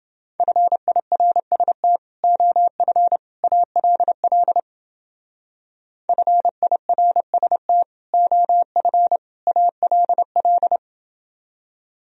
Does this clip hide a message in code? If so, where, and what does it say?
Yes: Morse code from 0.4 to 10.8 s, heard as a keyed tone, about 28 wpm, 700 Hz. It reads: FIRST OF ALL FIRST OF ALL